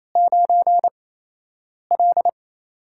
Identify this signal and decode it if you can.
Morse code, heard as a keyed tone, about 28 wpm, spaced as for 8 wpm, 700 Hz. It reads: 9L
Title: USA Call Signs - 8x Character Spacing - 28wpm